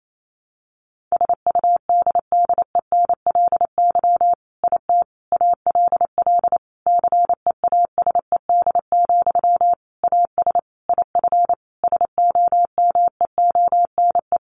Morse code, heard as a keyed tone, about 28 wpm, 700 Hz. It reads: SUDDENLY IT ALL CEASED, AS IF SOMEONE
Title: A Sound Of Thunder 28wpm